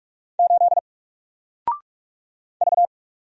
Morse code, heard as a keyed tone, about 44 wpm, 700 Hz. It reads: 8 V
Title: Single Character (Letter:Number:Punctuation:Prosign) 45wpm